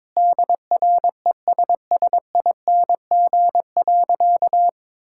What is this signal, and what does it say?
Morse code, heard as a keyed tone, about 21 wpm, 700 Hz. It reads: DRESSING.
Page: morse-code-ninja-practice-22wpm